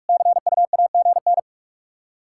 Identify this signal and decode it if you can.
Morse code, heard as a keyed tone, about 46 wpm, 700 Hz. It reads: CUAGN
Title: K1USN SST QSO Practice 45wpm